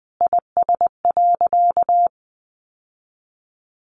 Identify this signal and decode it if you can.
Morse code, heard as a keyed tone, about 20 wpm, 700 Hz. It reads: IS.
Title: Sentences from Top 700 Words 20wpm